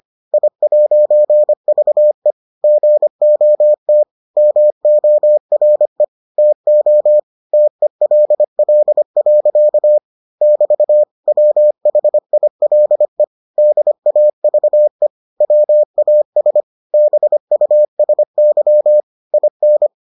Morse code, heard as a keyed tone, about 25 wpm, 600 Hz. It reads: I'VE GOT MORE TO TELL. <BT> WHILE DAVE WAS BUSY IN